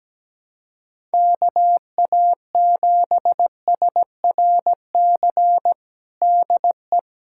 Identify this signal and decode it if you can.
Morse code, heard as a keyed tone, about 17 wpm, 700 Hz. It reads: KA7SRC DE
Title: USA Callsign DE Callsign - 6 Characters - 17wpm